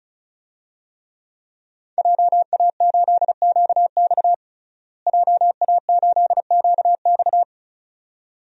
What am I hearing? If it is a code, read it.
Morse code, heard as a keyed tone, about 34 wpm, 700 Hz. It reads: JA8QX JA8QX
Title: DXCC-100 Call Signs - 5 Characters - Repeated 2x - 35wpm